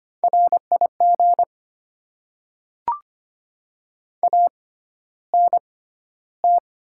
Morse code, heard as a keyed tone, about 25 wpm, spaced as for 13 wpm, 700 Hz. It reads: RIG ANT
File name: Top 20 QSO Elements - 6x Character Spacing - 25wpm